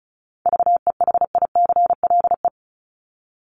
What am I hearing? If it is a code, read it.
Morse code, heard as a keyed tone, about 34 wpm, 700 Hz. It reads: VEHICLE
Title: Top 1751-2000 Words 35wpm